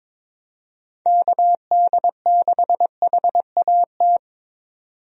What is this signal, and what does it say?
Morse code, heard as a keyed tone, about 22 wpm, 700 Hz. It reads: KD6HAT
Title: USA Call Signs - 6 Characters - 22wpm